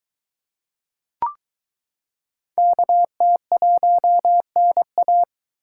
Morse code, heard as a keyed tone, about 22 wpm, 700 Hz. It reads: KT1NA